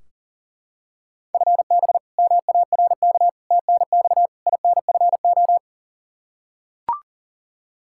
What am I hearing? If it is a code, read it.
Morse code, heard as a keyed tone, about 41 wpm, 700 Hz. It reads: FB MARK TNX INFO
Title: K1USN SST QSO Practice 40wpm